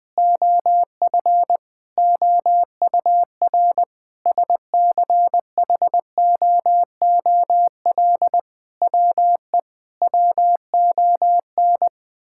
Morse code, heard as a keyed tone, about 21 wpm, 700 Hz. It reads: OF OUR SCHOOL WE WON
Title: Sentences with 8-Word Phrases - Spoken Beginning - 20wpm